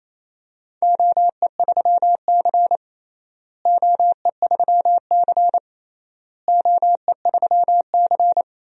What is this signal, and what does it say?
Morse code, heard as a keyed tone, about 28 wpm, 700 Hz. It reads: OE3C OE3C OE3C
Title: DXCC-100 Call Signs - 4 Characters - Repeated 3x - 28wpm